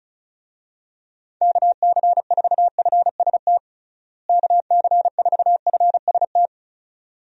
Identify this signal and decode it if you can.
Morse code, heard as a keyed tone, about 34 wpm, 700 Hz. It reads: KC4FST KC4FST
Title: USA Call Signs - 6 Characters - Repeated 2x - 35wpm